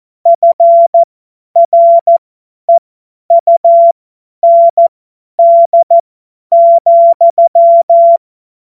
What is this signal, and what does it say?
Morse code, heard as a keyed tone, about 14 wpm, spaced as for 10 wpm, 680 Hz. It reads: FREUND,
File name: Morsekurs / Witze / Witz 04. Januar 2022, 50 F70